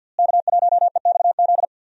Morse code, heard as a keyed tone, about 52 wpm, 700 Hz. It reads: K1EXZ
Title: USA Call Signs - 5 Characters - Repeated 3x - 50wpm